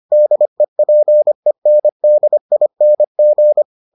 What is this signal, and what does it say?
Morse code, heard as a keyed tone, about 25 wpm, 600 Hz. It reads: DEPENDING